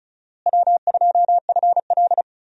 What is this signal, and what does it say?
Morse code, heard as a keyed tone, about 34 wpm, 700 Hz. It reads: W2FL
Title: USA Call Signs - 4 Characters - Repeated 2x - 35wpm